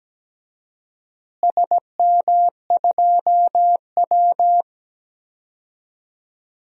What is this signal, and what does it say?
Morse code, heard as a keyed tone, about 17 wpm, 700 Hz. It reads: SM2W
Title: DXCC-100 Call Signs - 4 Characters - Repeated 3x - 17wpm